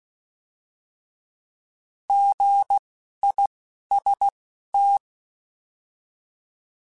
Morse code, heard as a keyed tone, about 16 wpm, spaced as for 11 wpm, 800 Hz. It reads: GIST
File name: Alphabetical / Alphabetical mp3s / Gs @10